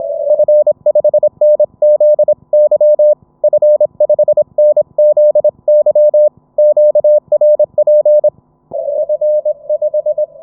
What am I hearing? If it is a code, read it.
Morse code, heard as a keyed tone, about 22 wpm, 600 Hz. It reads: F5NZY F5NZY QRP F5
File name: F5NZY-QRP-5W-HS0ZME-20M-06-01-2019